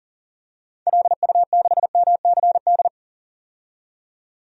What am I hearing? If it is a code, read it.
Morse code, heard as a keyed tone, about 40 wpm, 700 Hz. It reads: LU6MCD
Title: DXCC-100 Call Signs - 6 Characters - 40wpm